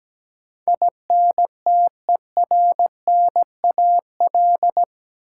Morse code, heard as a keyed tone, about 17 wpm, 700 Hz. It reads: INTERNAL
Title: Sentences from Top 1400 Words - New Word Drill - 17wpm